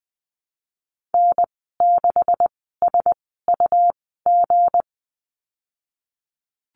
Morse code, heard as a keyed tone, about 20 wpm, spaced as for 15 wpm, 700 Hz. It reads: N6SUG
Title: 186 Lesson 30g - Letter b - Call Signs 20wpm